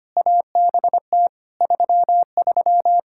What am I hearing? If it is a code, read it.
Morse code, heard as a keyed tone, about 25 wpm, 700 Hz. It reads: ABT 33
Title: Practice QSO - Exchange 3 - More Info - 25wpm